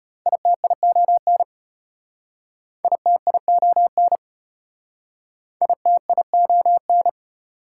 Morse code, heard as a keyed tone, about 33 wpm, 700 Hz. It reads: ITION ITION ITION